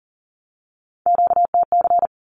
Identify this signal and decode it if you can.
Morse code, heard as a keyed tone, about 40 wpm, 700 Hz. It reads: QTC
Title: Q-Codes 40wpm